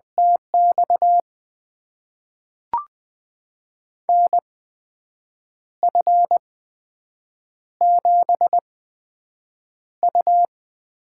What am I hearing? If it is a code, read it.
Morse code, heard as a keyed tone, about 20 wpm, spaced as for 7 wpm, 700 Hz. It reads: TX NF7U
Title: USA Call Signs - 4 Characters - 8x Character Spacing - 20wpm